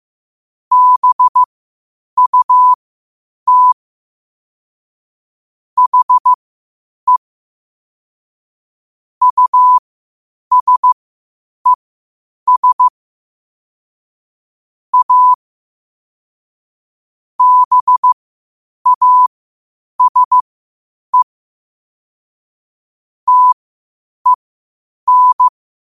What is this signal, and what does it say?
Morse code, heard as a keyed tone, about 15 wpm, spaced as for 8 wpm, 1000 Hz. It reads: BUT HE USES A BASE TEN
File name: qotdfiles / 8wpm